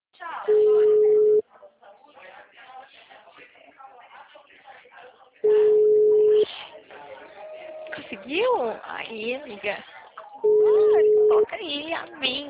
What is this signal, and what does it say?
Ringback tone (Europe, 425 Hz), with speech.